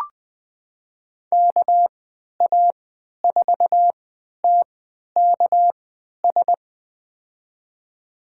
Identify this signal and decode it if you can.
Morse code, heard as a keyed tone, about 20 wpm, spaced as for 11 wpm, 700 Hz. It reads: KA4TKS
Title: USA Call Signs - 6 Characters - 3x Character Spacing - 20wpm